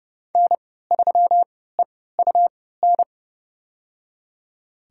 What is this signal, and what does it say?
Morse code, heard as a keyed tone, about 29 wpm, spaced as for 17 wpm, 700 Hz. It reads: N3EUN
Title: USA Call Signs - 5 Characters - 3x Character Spacing - 30wpm